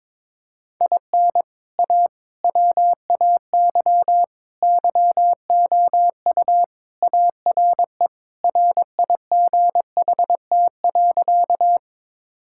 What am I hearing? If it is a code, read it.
Morse code, heard as a keyed tone, about 22 wpm, 700 Hz. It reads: IN A WAY YOU ARE RIGHT.
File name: practice / 22wpm / Sentences with 3-Word Phrases - 22wpm